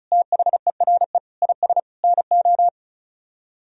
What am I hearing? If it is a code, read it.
Morse code, heard as a keyed tone, about 44 wpm, 700 Hz. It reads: THERE IS NO